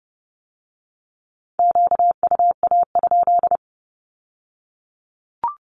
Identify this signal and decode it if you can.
Morse code, heard as a keyed tone, about 30 wpm, 700 Hz. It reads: QUA?
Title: Q-Codes with Spoken Meaning - 30wpm